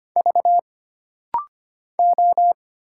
Morse code, heard as a keyed tone, about 25 wpm, 700 Hz. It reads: V O
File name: Single Character (Letter:Number:Punctuation:Prosign) - Rapid-Fire - 25wpm